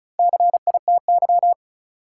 Morse code, heard as a keyed tone, about 35 wpm, 700 Hz. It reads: CITY
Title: Sets of 2 Words - New Word Drill - 35wpm